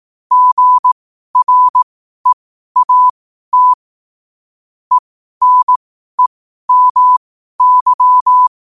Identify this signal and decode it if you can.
Morse code, heard as a keyed tone, about 18 wpm, spaced as for 12 wpm, 1000 Hz. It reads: GREAT ENEMY
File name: qotdfiles / 12wpm